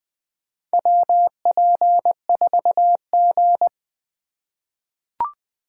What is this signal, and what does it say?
Morse code, heard as a keyed tone, about 20 wpm, 700 Hz. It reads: WP4G
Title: USA Call Signs - 4 Characters - 4x Character Spacing - 20wpm